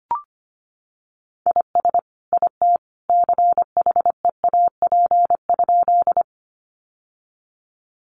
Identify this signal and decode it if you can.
Morse code, heard as a keyed tone, about 25 wpm, 700 Hz. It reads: IS IT CHEAP?
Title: Sentences from Top 1200 Words 25wpm